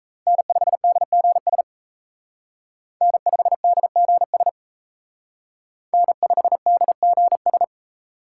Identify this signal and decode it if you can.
Morse code, heard as a keyed tone, about 35 wpm, 700 Hz. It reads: N5DGS N5DGS N5DGS
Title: Speed Racing - USA 1x3 Call Signs 28wpm